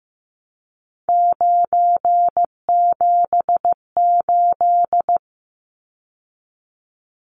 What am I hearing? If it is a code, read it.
Morse code, heard as a keyed tone, about 15 wpm, 700 Hz. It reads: 978
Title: Numbers - 3-Digit 15wpm